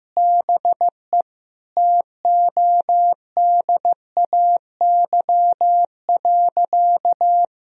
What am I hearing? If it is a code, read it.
Morse code, heard as a keyed tone, about 15 wpm, 700 Hz. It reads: BE TODAY.
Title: Sentences from Top 1000 Words - Review Part 4 - 15wpm